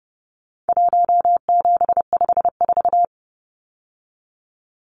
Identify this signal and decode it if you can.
Morse code, heard as a keyed tone, about 30 wpm, 700 Hz. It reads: 1754
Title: Numbers - Contest Style - 4-Digit 30wpm